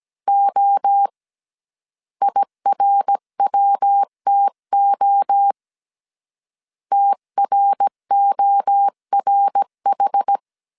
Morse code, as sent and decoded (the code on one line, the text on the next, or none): --- / .. .-. .-- - --- / - .-. --- .-. ....
O IRWTO TRORH